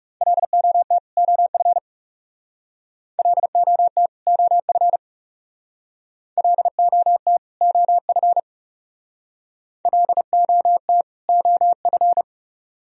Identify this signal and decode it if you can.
Morse code, heard as a keyed tone, about 36 wpm, 700 Hz. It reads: LOT OF LOT OF LOT OF LOT OF
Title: Speed-Racing - Sets of 2 Words 30wpm